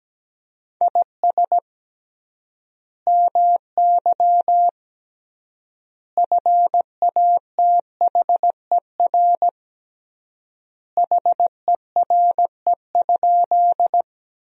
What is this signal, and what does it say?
Morse code, heard as a keyed tone, about 17 wpm, 700 Hz. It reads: IS MY FATHER HERE?
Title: 3x Word-Spacing - Sentences from Top 300 Words - 17wpm